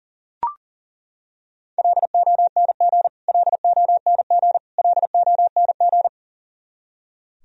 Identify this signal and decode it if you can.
Morse code, heard as a keyed tone, about 40 wpm, 700 Hz. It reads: LONG LONG LONG
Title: Top 100 Words Repeated 40wpm